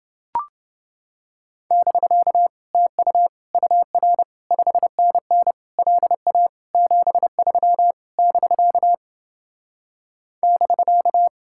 Morse code, heard as a keyed tone, about 29 wpm, 700 Hz. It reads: <BK> TU UR 5NN LA 73 <BK> <BK>